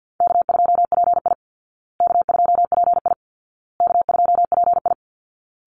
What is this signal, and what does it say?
Morse code, heard as a keyed tone, about 50 wpm, 700 Hz. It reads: K2PI K2PI K2PI
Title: Easy Call Signs Repeated 3x - 50wpm